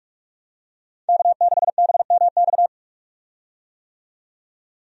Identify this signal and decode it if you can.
Morse code, heard as a keyed tone, about 43 wpm, 700 Hz. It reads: K6BMX
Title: USA Call Signs - 5 Characters - 45wpm